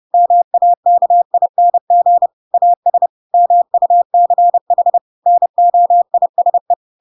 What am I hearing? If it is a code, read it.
Morse code, heard as a keyed tone, about 30 wpm, 700 Hz. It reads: MAKING AS MUCH NOISE
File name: Pinocchio / Pinocchio ch3 30 30 ik1hge